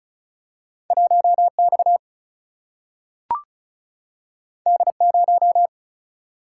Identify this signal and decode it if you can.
Morse code, heard as a keyed tone, about 35 wpm, 700 Hz. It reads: 1X D0
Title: Single-Letter Single-Number 35wpm